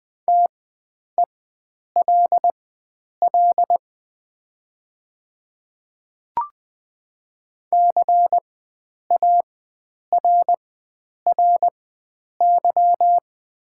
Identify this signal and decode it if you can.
Morse code, heard as a keyed tone, about 20 wpm, spaced as for 9 wpm, 700 Hz. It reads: TELL CARRY
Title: 4x Character Spacing - Top 300 Words - 20wpm